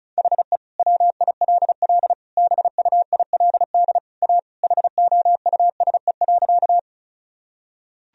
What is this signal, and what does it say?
Morse code, heard as a keyed tone, about 34 wpm, 700 Hz. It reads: HE WILL BUILD A HOUSE.
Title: Sentences from Top 1000 Words - Review Part 1 - 35wpm